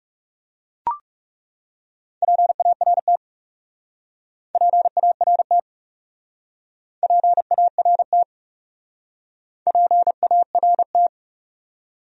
Morse code, heard as a keyed tone, about 41 wpm, 700 Hz. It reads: PART PART PART PART